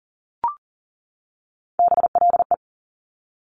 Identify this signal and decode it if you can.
Morse code, heard as a keyed tone, about 41 wpm, 700 Hz. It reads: BLE
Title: Top Three Letter Combinations in Words 40wpm